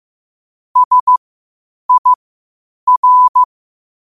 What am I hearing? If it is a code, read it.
Morse code, heard as a keyed tone, about 15 wpm, spaced as for 8 wpm, 1000 Hz. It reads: SIR
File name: qotdfiles / 8wpm